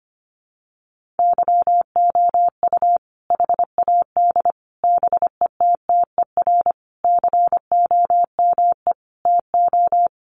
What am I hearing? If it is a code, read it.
Morse code, heard as a keyed tone, about 25 wpm, 700 Hz. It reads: YOU HAD BETTER COME TO
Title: Sentences from Top 400 Words 25wpm